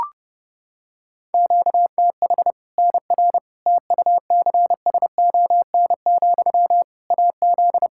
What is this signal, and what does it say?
Morse code, heard as a keyed tone, about 30 wpm, 700 Hz. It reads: QTH NR TUCSON, AZ
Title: Practice QSO - Exchange 1 - Beginning - 30wpm